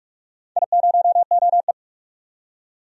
Morse code, heard as a keyed tone, about 44 wpm, 700 Hz. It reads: I0OE
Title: DXCC-100 Call Signs - 4 Characters - 45wpm